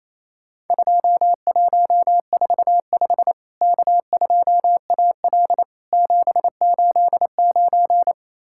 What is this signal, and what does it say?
Morse code, heard as a keyed tone, about 28 wpm, 700 Hz. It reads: 2145 K2AL 789
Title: Worked All Europe DX Contest - QTC Exchange - 28wpm